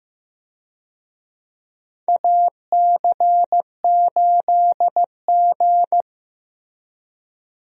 Morse code, heard as a keyed tone, about 15 wpm, 700 Hz. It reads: AC8G
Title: USA Call Signs - 4 Characters - Repeated 2x - 15wpm